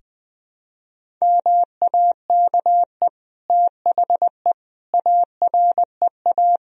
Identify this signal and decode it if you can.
Morse code, heard as a keyed tone, about 20 wpm, 700 Hz. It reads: MAKE THE AREA
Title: Sentences from Top 1000 Words - Review Part 2 - 20wpm